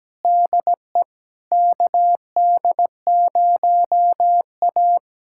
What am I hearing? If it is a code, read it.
Morse code, heard as a keyed tone, about 17 wpm, 700 Hz. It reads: DE KD0A
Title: USA Callsign DE Callsign - 4 Characters - 17wpm